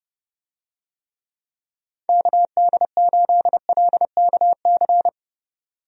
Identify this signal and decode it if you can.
Morse code, heard as a keyed tone, about 32 wpm, 700 Hz. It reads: KD8LKC
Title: USA Call Signs - 6 Characters - 30wpm